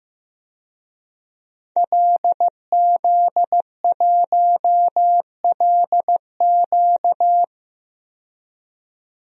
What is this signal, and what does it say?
Morse code, heard as a keyed tone, about 15 wpm, 700 Hz. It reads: LZ1LQ
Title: DXCC-100 Call Signs - 5 Characters - 15wpm